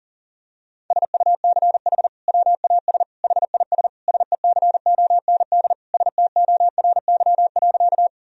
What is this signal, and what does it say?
Morse code, heard as a keyed tone, about 37 wpm, 700 Hz. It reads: SUCH WAS HIS SECOND STORY.
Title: Sentences from Top 1000 Words - Review Part 1 - 40wpm